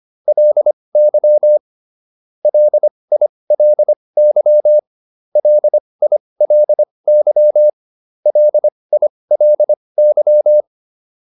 Morse code, heard as a keyed tone, about 25 wpm, spaced as for 20 wpm, 600 Hz. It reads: LY LILY LILY LILY